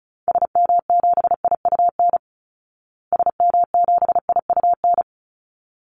Morse code, heard as a keyed tone, about 34 wpm, 700 Hz. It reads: SM7IUN SM7IUN
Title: Call Signs Repeated 3x - 35wpm